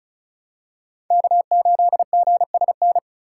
Morse code, heard as a keyed tone, about 34 wpm, 700 Hz. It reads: K8GSN
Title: USA Call Signs - 5 Characters - 3x Character Spacing - 35wpm